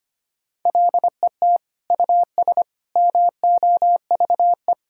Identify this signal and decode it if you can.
Morse code, heard as a keyed tone, about 25 wpm, 700 Hz. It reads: LET US MOVE
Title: Sentences from Top 1000 Words - Review Part 2 - 25wpm